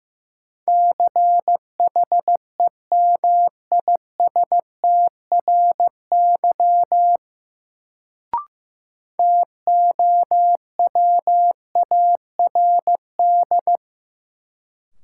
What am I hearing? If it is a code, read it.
Morse code, heard as a keyed tone, about 15 wpm, 700 Hz. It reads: CHEMISTRY TOWARD
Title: Sentences from Top 1000 Words - Review Part 6 - New Word Drill - 15wpm